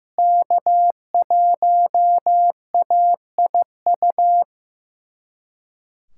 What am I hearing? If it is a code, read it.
Morse code, heard as a keyed tone, about 15 wpm, 700 Hz. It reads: K1AIU
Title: USA Callsign DE Callsign - 5 Characters - 15wpm